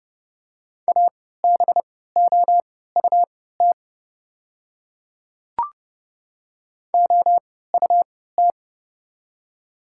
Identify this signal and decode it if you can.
Morse code, heard as a keyed tone, about 32 wpm, spaced as for 17 wpm, 700 Hz. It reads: ABOUT OUT
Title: 3x Character Spacing - Top 100 Words - 30wpm